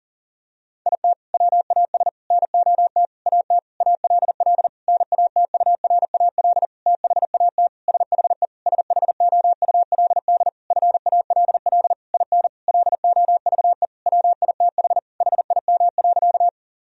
Morse code, heard as a keyed tone, about 40 wpm, 700 Hz. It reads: IT WAS NOT AT ALL NATURAL THAT SHE SHOULD FALL IN LOVE WITH HIM.
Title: Sentences from Top 1000 Words - Review Part 4 - 40wpm